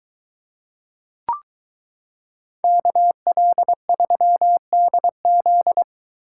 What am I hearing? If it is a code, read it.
Morse code, heard as a keyed tone, about 22 wpm, 700 Hz. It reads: KL3DZ